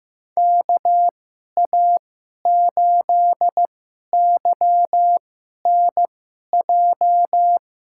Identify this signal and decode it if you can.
Morse code, heard as a keyed tone, about 15 wpm, spaced as for 11 wpm, 700 Hz. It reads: KA8YNJ
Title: USA Call Signs - 6 Characters - 2x Character Spacing - 15wpm